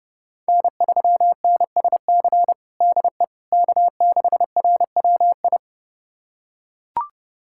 Morse code, heard as a keyed tone, about 30 wpm, 700 Hz. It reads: N3NSC DE K6RWI
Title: USA Callsign DE Callsign - 5 Characters - 30wpm